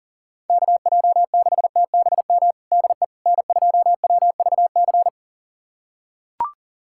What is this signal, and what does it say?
Morse code, heard as a keyed tone, about 40 wpm, 700 Hz. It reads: KJ6TBM DE N2WVC
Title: USA Callsign DE Callsign 40wpm